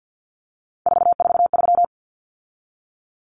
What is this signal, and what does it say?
Morse code, heard as a keyed tone, about 48 wpm, 700 Hz. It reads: 443
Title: Numbers - 3-Digit 50wpm